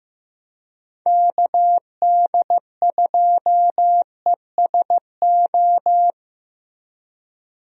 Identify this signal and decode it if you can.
Morse code, heard as a keyed tone, about 16 wpm, 700 Hz. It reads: KD2ESO
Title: USA Call Signs - 6 Characters - Repeated 3x - 15wpm